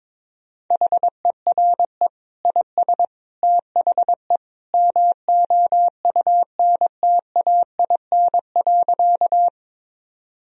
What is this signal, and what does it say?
Morse code, heard as a keyed tone, about 22 wpm, 700 Hz. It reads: HERE IS THE MOUNTAIN.
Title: Sentences from Top 1000 Words - Review Part 1 - 22wpm